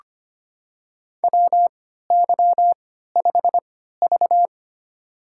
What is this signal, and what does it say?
Morse code, heard as a keyed tone, about 25 wpm, spaced as for 14 wpm, 700 Hz. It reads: WY5V
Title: USA Call Signs - 4 Characters - 3x Character Spacing - 25wpm